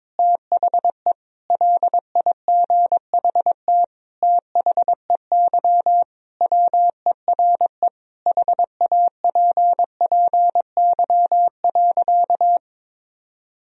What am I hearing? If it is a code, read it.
Morse code, heard as a keyed tone, about 22 wpm, 700 Hz. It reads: THE LIGHT THEY WERE HAPPY.
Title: Sentences from Top 1000 Words - Review Part 1 - 22wpm